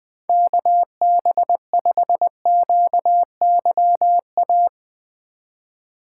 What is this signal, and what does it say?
Morse code, heard as a keyed tone, about 20 wpm, 700 Hz. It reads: KB5QYA
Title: USA Call Signs - 6 Characters - 8x Character Spacing - 20wpm